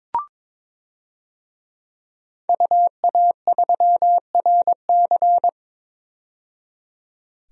Morse code, heard as a keyed tone, about 22 wpm, 700 Hz. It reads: UA3RC
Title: DXCC-100 Call Signs - 5 Characters - 22wpm